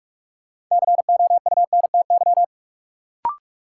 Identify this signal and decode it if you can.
Morse code, heard as a keyed tone, about 46 wpm, 700 Hz. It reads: COUNTY
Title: Sentences from Top 1200 Words - New Word Drill Alt - 45wpm